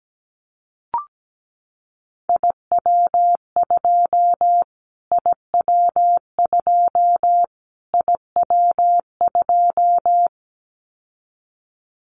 Morse code, heard as a keyed tone, about 17 wpm, 700 Hz. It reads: IW2 IW2 IW2
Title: Callsign Prefixes for Top 25 DXCC - Repeated 3x - 17wpm